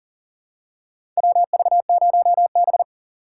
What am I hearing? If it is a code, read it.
Morse code, heard as a keyed tone, about 40 wpm, 700 Hz. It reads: WV0B
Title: USA Call Signs - 4 Characters - 3x Character Spacing - 40wpm